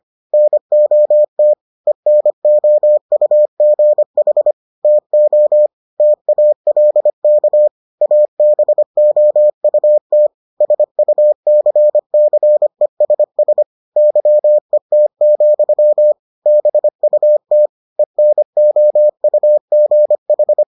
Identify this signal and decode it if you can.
Morse code, heard as a keyed tone, about 25 wpm, 600 Hz. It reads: NOT ENOUGH TO TALK ABOUT SUCCESS YET, BUT ENOUGH